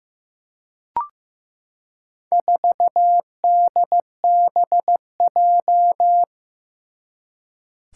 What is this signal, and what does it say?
Morse code, heard as a keyed tone, about 16 wpm, 700 Hz. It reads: 4DBJ
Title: Four Challenging Letter-Number Combinations 15wpm